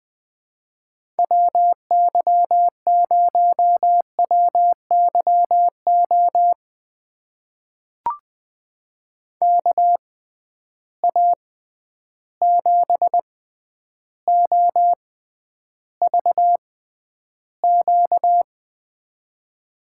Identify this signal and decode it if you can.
Morse code, heard as a keyed tone, about 20 wpm, spaced as for 10 wpm, 700 Hz. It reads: WY0WYO KA7OVQ
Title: USA Call Signs - 6 Characters - 6x Character Spacing - 20wpm